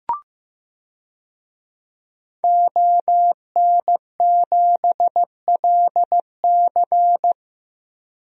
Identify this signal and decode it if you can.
Morse code, heard as a keyed tone, about 15 wpm, 700 Hz. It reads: ON7LC
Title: DXCC-100 Call Signs - 5 Characters - 15wpm